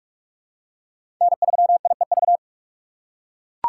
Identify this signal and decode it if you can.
Morse code, heard as a keyed tone, about 42 wpm, 700 Hz. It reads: N3IEV
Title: USA Call Signs - 5 Characters - 2x Character Spacing - 45wpm